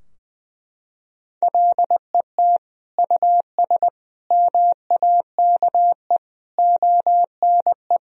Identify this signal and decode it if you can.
Morse code, heard as a keyed tone, about 21 wpm, 700 Hz. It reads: LET US MAKE ONE
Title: Sentences from Top 1000 Words - Review Part 4 - 20wpm